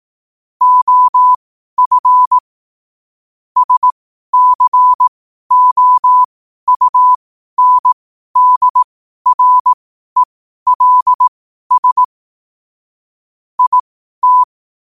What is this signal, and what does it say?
Morse code, heard as a keyed tone, about 18 wpm, spaced as for 12 wpm, 1000 Hz. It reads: OF SCOUNDRELS IT
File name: qotdfiles / 12wpm